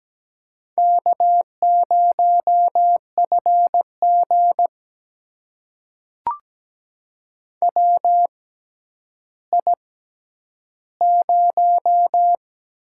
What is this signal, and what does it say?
Morse code, heard as a keyed tone, about 17 wpm, spaced as for 10 wpm, 700 Hz. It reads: K0FG WI0
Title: USA Call Signs - 4 Characters - 6x Character Spacing - 17wpm